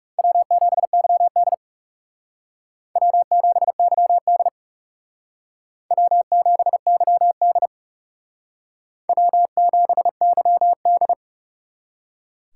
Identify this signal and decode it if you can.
Morse code, heard as a keyed tone, about 36 wpm, 700 Hz. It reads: W7YD W7YD W7YD W7YD
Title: Speed Racing - USA 1x2 Call Signs 30wpm